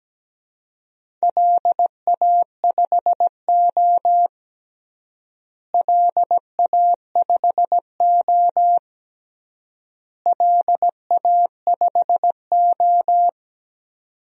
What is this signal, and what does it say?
Morse code, heard as a keyed tone, about 17 wpm, 700 Hz. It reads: LA5O LA5O LA5O
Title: DXCC-100 Call Signs - 4 Characters - Repeated 3x - 17wpm